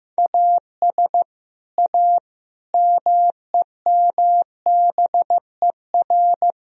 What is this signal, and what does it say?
Morse code, heard as a keyed tone, about 15 wpm, 700 Hz. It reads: AS A MEMBER
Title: Speed-Racing - Sets of 5 Words 15wpm